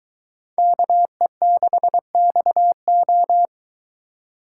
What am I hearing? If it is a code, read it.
Morse code, heard as a keyed tone, about 23 wpm, 700 Hz. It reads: KE6XO